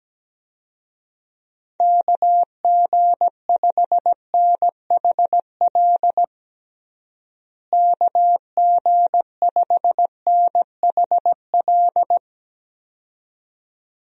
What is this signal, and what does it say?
Morse code, heard as a keyed tone, about 17 wpm, 700 Hz. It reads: KG5NHL KG5NHL
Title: USA Call Signs - 6 Characters - Repeated 2x - 17wpm